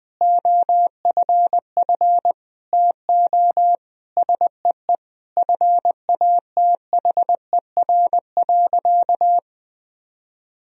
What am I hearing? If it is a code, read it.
Morse code, heard as a keyed tone, about 20 wpm, 700 Hz. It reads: OFF TO SEE FATHER.
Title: Sentences from Top 1000 Words - Review Part 1 - 20wpm